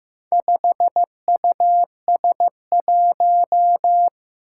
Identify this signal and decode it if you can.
Morse code, heard as a keyed tone, about 15 wpm, 700 Hz. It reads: 5US1
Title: Four Challenging Letter-Number Combinations 15wpm